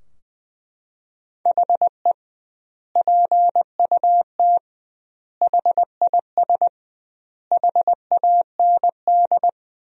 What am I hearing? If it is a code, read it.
Morse code, heard as a keyed tone, about 21 wpm, 700 Hz. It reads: HE PUT HIS HAND
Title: 2x Word-Spacing - Sentences from Top 400 Words - 20wpm